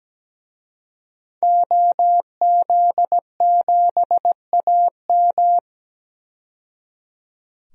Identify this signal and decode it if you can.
Morse code, heard as a keyed tone, about 17 wpm, 700 Hz. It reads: OZ7AM
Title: DXCC-100 Call Signs - 5 Characters - 17wpm